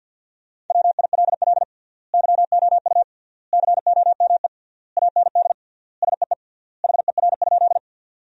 Morse code, heard as a keyed tone, about 50 wpm, 700 Hz. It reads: WILL YOU COME AND SEE HER?
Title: 3x Word-Spacing - Sentences from Top 100 Words - 50wpm